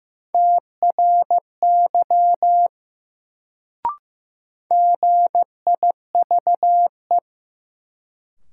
Morse code, heard as a keyed tone, about 15 wpm, 700 Hz. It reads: TRY GIVE
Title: Sentences from Top 200 Words - New Word Drill - 15wpm